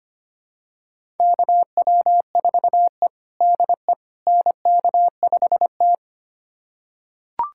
Morse code, heard as a keyed tone, about 25 wpm, 700 Hz. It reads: KW4E DE NK5T
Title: USA Callsign DE Callsign - 4 Characters - 25wpm